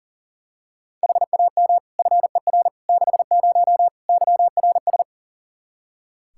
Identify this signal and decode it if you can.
Morse code, heard as a keyed tone, about 37 wpm, 700 Hz. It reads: HAM FER 60 YRS
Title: K1USN SST QSO Practice 40wpm